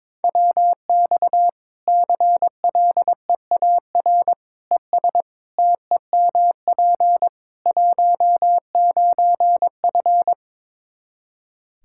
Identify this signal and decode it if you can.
Morse code, heard as a keyed tone, about 22 wpm, 700 Hz. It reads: WX CLEAR ES TEMP 19F
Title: Practice QSO - Exchange 2 - Info - 22wpm